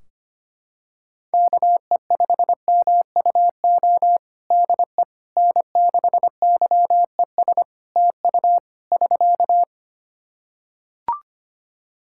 Morse code, heard as a keyed tone, about 25 wpm, 700 Hz. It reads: KE5MUO DE N6YES TU <SK>
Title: K1USN SST QSO Practice 25wpm